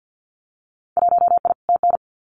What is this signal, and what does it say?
Morse code, heard as a keyed tone, about 50 wpm, 700 Hz. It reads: 1I TN
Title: Winter Field Day Exchanges 50wpm